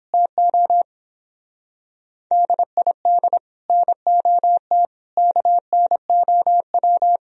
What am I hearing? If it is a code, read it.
Morse code, heard as a keyed tone, about 26 wpm, 700 Hz. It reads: TO DID NOT KNOW
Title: Speed-Racing - Sets of 5 Words 22wpm